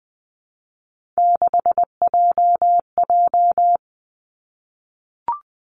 Morse code, heard as a keyed tone, about 20 wpm, 700 Hz. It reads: 6JJ
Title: Three Challenging Letter-Number Combinations 20wpm